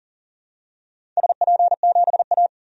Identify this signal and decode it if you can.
Morse code, heard as a keyed tone, about 40 wpm, 700 Hz. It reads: SP7A
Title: DXCC-100 Call Signs - 4 Characters - 40wpm